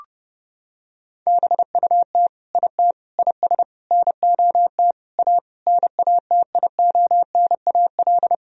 Morse code, heard as a keyed tone, about 30 wpm, 700 Hz. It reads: BUT IT IS NOT A NATIONAL
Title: Sentences from Top 1000 Words - Review Part 4 - 30wpm